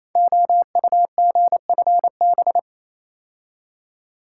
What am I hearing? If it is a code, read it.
Morse code, heard as a keyed tone, about 28 wpm, 700 Hz. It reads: OUGFB
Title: Code Groups - 5 Letters - 28wpm